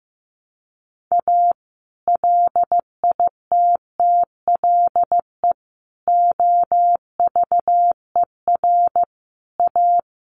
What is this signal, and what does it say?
Morse code, heard as a keyed tone, about 15 wpm, 700 Hz. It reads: A LITTLE OVER A
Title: Sentences from Top 1300 Words 15wpm